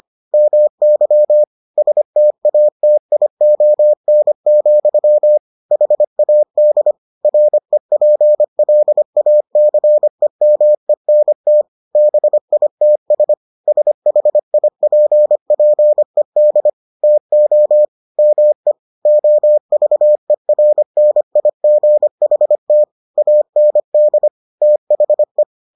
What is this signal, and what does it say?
Morse code, heard as a keyed tone, about 25 wpm, 600 Hz. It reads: MY STATION, HAD REPLACEMENT BITS SHIPPED TO ME OVERNIGHT AND THE